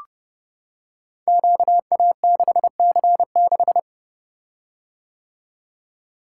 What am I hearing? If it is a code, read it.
Morse code, heard as a keyed tone, about 30 wpm, 700 Hz. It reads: QA6C6
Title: Code Groups - 5 Letter:Numbers - 30wpm